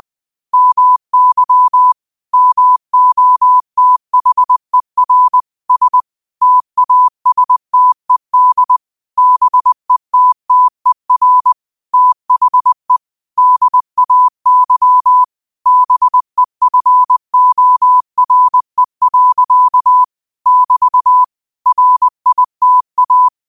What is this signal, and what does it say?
Morse code, heard as a keyed tone, about 20 wpm, 1000 Hz. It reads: MY MOTHER S TASTED BETTER THE DAY BEFORE. <BT> RITA
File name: qotdfiles / 20wpm